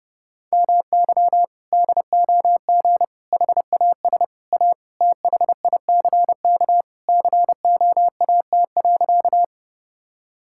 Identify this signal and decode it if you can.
Morse code, heard as a keyed tone, about 32 wpm, 700 Hz. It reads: MY DOG HAS A THICK COAT.
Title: Sentences from Top 1600 Words 30wpm